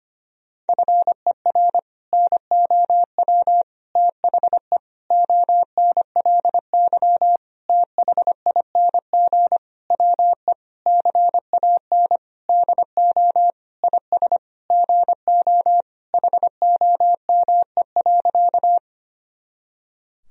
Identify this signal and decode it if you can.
Morse code, heard as a keyed tone, about 25 wpm, 700 Hz. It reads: FER NOW THE ONLY THING WE CAN DO IS GO HOME.
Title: Sentences from Top 200 Words and Abbreviations 25wpm